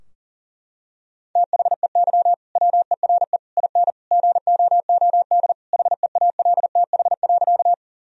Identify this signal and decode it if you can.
Morse code, heard as a keyed tone, about 40 wpm, 700 Hz. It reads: THEY WERE IN GOOD HEALTH.
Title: Sentences from Top 1000 Words - Review Part 3 - 40wpm